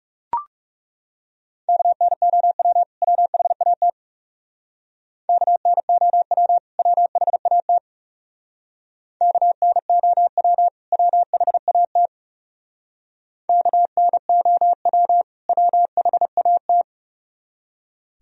Morse code, heard as a keyed tone, about 38 wpm, 700 Hz. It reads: KNOW WHAT KNOW WHAT KNOW WHAT KNOW WHAT